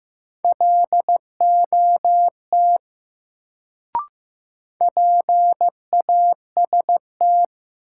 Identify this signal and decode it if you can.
Morse code, heard as a keyed tone, about 15 wpm, 700 Hz. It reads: LOT PAST
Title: Sets of 2 Words - New Word Drill - 15wpm